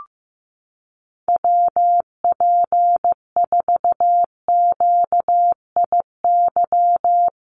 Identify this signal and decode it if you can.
Morse code, heard as a keyed tone, about 15 wpm, 700 Hz. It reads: WP4QIY
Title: USA Call Signs 15wpm